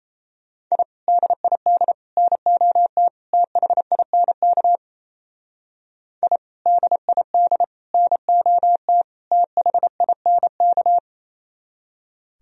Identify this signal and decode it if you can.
Morse code, heard as a keyed tone, about 35 wpm, 700 Hz. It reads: I DID NOT THINK I DID NOT THINK